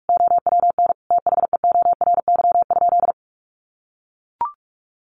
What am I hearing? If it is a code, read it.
Morse code, heard as a keyed tone, about 44 wpm, 700 Hz. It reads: OWN THEORY?
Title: Sentences from Top 1000 Words 45wpm